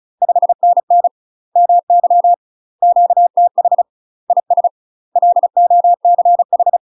Morse code, heard as a keyed tone, about 34 wpm, 700 Hz. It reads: 5NN MY QTH IS LOCH
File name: UK MASTER QSOs TRAINING / 35WPM / UK MASTER QSOs 0037